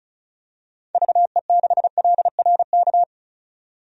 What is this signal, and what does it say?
Morse code, heard as a keyed tone, about 34 wpm, 700 Hz. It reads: VE6LRK
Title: DXCC-100 Call Signs - 6 Characters - 35wpm